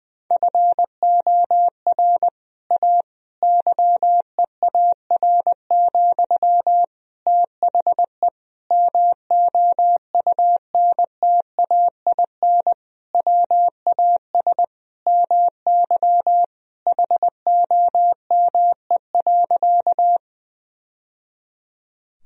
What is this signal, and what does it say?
Morse code, heard as a keyed tone, about 21 wpm, 700 Hz. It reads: FOR A YEAR, THE MOUNTAIN WAS MY HOME.
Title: Sentences from Top 1000 Words - Review Part 1 - 20wpm